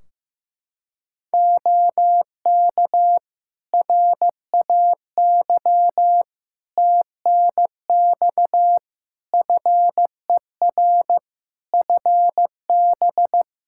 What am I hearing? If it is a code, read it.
Morse code, heard as a keyed tone, about 15 wpm, 700 Hz. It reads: OK RAY TNX FER FB
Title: Practice QSO - Exchange 4 - Ending - 15wpm